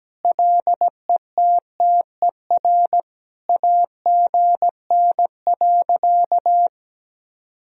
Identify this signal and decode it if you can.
Morse code, heard as a keyed tone, about 17 wpm, 700 Hz. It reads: LETTER AGN.
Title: Sentences from Top 300 Words and Abbreviations 17wpm